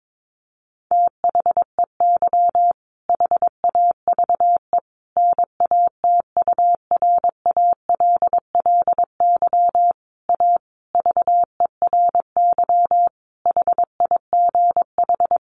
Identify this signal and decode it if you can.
Morse code, heard as a keyed tone, about 22 wpm, 700 Hz. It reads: THEY HAVE NATURALLY A VERY HIGH
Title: Sentences from Top 1300 Words 22wpm